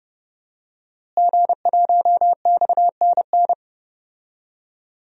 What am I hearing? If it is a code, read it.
Morse code, heard as a keyed tone, about 30 wpm, 700 Hz. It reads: G1XNN
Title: DXCC-100 Call Signs - 5 Characters - Repeated 2x - 30wpm